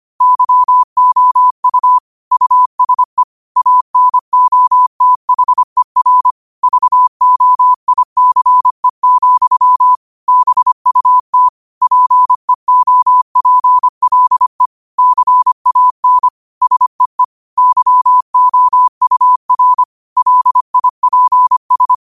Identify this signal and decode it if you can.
Morse code, heard as a keyed tone, about 25 wpm, 1000 Hz. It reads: YOU USE ANOTHER VOICE, BUT PEOPLE CAN SEE YOUR LIPS